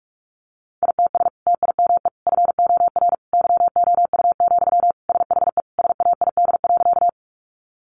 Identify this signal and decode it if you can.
Morse code, heard as a keyed tone, about 44 wpm, 700 Hz. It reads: ITS TIME FOR YOU, SHE SAID.
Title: Sentences from Top 600 Words 45wpm